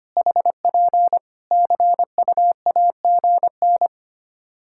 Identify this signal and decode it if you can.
Morse code, heard as a keyed tone, about 25 wpm, 700 Hz. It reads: HP CUAGN
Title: Practice QSO - Exchange 4 - Ending - 25wpm